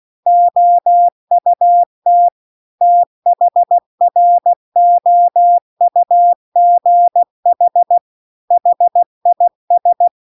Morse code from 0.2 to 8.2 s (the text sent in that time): OUT THROUGH